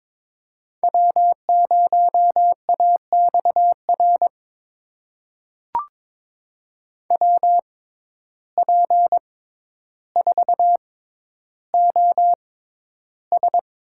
Morse code, heard as a keyed tone, about 21 wpm, spaced as for 11 wpm, 700 Hz. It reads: W0AXR WP4OS
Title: USA Call Signs - 5 Characters - 6x Character Spacing - 22wpm